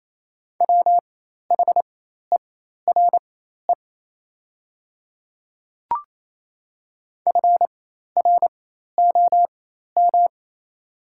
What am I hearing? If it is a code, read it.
Morse code, heard as a keyed tone, about 28 wpm, spaced as for 13 wpm, 690 Hz. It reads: WHERE FROM
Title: 4x Character Spacing - Top 300 Words - 28wpm